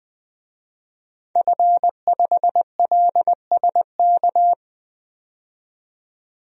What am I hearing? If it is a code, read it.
Morse code, heard as a keyed tone, about 20 wpm, 700 Hz. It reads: F5LSK
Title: DXCC-100 Call Signs - 5 Characters - 20wpm